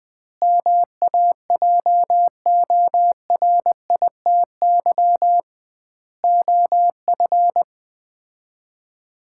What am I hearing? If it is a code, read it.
Morse code, heard as a keyed tone, about 20 wpm, 700 Hz. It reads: MAJORITY OF
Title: 2x Word-Spacing - Sets of 4 Words - 20wpm